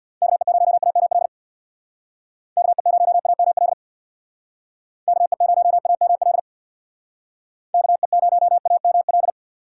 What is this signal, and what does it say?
Morse code, heard as a keyed tone, about 61 wpm, 700 Hz. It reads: KE0AML KE0AML KE0AML KE0AML
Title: Speed Racing - USA 2x3 Call Signs 50wpm